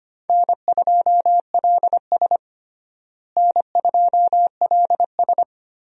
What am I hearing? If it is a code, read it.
Morse code, heard as a keyed tone, about 25 wpm, 700 Hz. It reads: N2LS N2LS
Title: USA Call Signs - 4 Characters - Repeated 2x - 25wpm